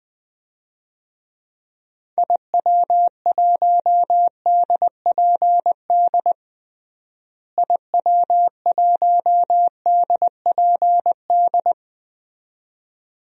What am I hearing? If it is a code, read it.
Morse code, heard as a keyed tone, about 21 wpm, 700 Hz. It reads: IW1DPD IW1DPD
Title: DXCC-100 Call Signs - 6 Characters - Repeated 2x - 20wpm